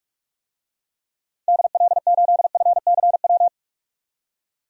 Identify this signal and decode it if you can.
Morse code, heard as a keyed tone, about 46 wpm, 700 Hz. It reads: DL8FCW
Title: DXCC-100 Call Signs - 6 Characters - 45wpm